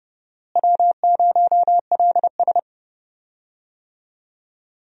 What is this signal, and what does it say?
Morse code, heard as a keyed tone, about 30 wpm, 700 Hz. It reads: W0LS
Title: USA Call Signs - 4 Characters - Repeated 2x - 30wpm